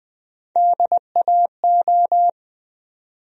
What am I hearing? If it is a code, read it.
Morse code, heard as a keyed tone, about 20 wpm, 700 Hz. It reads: DAO
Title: USA Call Sign Postfixes - 3 Letters 20wpm